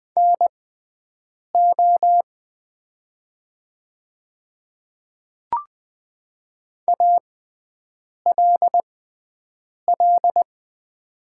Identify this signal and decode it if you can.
Morse code, heard as a keyed tone, about 20 wpm, spaced as for 7 wpm, 700 Hz. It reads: NO ALL
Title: 6x Character Spacing - Top 100 Words - 20wpm